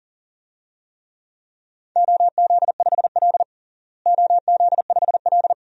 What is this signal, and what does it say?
Morse code, heard as a keyed tone, about 40 wpm, 700 Hz. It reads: OZ5L OZ5L
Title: DXCC-100 Call Signs - 4 Characters - Repeated 3x - 40wpm